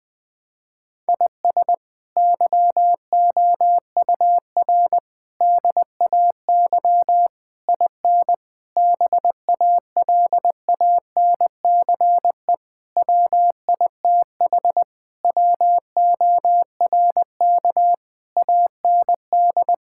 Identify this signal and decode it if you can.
Morse code, heard as a keyed tone, about 20 wpm, 700 Hz. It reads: IS YOUR DAY IN BALANCE WITH WORK AND
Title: Sentences from Top 1000 Words - Review Part 6 - 20wpm